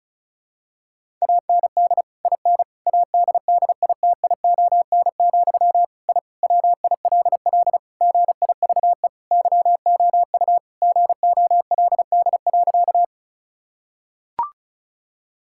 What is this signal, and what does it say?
Morse code, heard as a keyed tone, about 34 wpm, 700 Hz. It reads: AND IN ADDITION, I WILL GIVE YOU GOLD.
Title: Sentences from Top 1000 Words - Review Part 6 - 35wpm